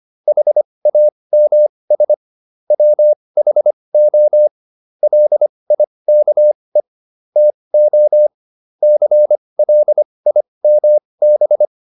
Morse code, heard as a keyed tone, about 25 wpm, spaced as for 20 wpm, 600 Hz. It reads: HAMS WHO LIKE TO CLIMB